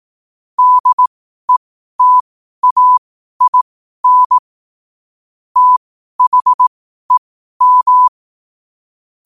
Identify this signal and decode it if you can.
Morse code, heard as a keyed tone, about 18 wpm, spaced as for 12 wpm, 1000 Hz. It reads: DETAIN THEM
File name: qotdfiles / 12wpm